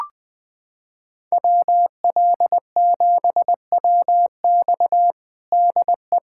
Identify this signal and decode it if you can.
Morse code, heard as a keyed tone, about 21 wpm, 700 Hz. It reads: WL7WX DE
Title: USA Callsign DE Callsign 20wpm